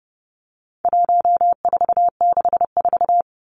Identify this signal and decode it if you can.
Morse code, heard as a keyed tone, about 30 wpm, 700 Hz. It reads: 1464
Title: Numbers - 4-Digit 30wpm